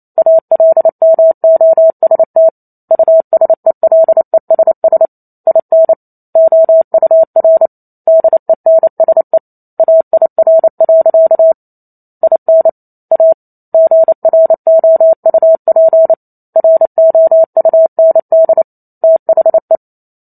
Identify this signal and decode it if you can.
Morse code, heard as a keyed tone, about 28 wpm, 650 Hz. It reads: ALMOST USELESS IN OUR DENSE AIR. IN A GROUP ROUND THE